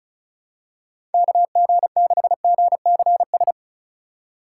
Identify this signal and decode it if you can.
Morse code, heard as a keyed tone, about 34 wpm, 700 Hz. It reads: KG6GCS
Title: USA Call Signs - 6 Characters - 2x Character Spacing - 35wpm